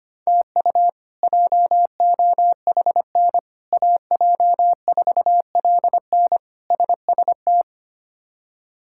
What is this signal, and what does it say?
Morse code, heard as a keyed tone, about 25 wpm, 700 Hz. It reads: TU JOHN AJ4LN SST
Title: K1USN SST QSO Practice - 25wpm